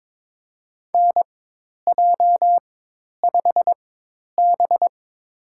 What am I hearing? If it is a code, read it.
Morse code, heard as a keyed tone, about 22 wpm, spaced as for 10 wpm, 700 Hz. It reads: NJ5B
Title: USA Call Signs - 4 Characters - 4x Character Spacing - 22wpm